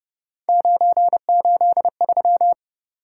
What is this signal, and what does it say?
Morse code, heard as a keyed tone, about 31 wpm, 700 Hz. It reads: 983